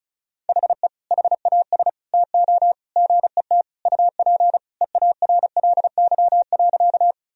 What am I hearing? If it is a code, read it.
Morse code, heard as a keyed tone, about 36 wpm, 700 Hz. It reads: HE HAS TO GET UP EARLY.
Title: Sentences from Top 1000 Words - Review Part 2 - 35wpm